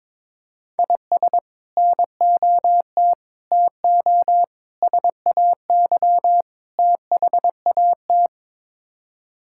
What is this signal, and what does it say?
Morse code, heard as a keyed tone, about 22 wpm, 700 Hz. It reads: IS NOT TO SAY THAT
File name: practice / 22wpm / Build A Phrase - 5-Word Phrases - 22wpm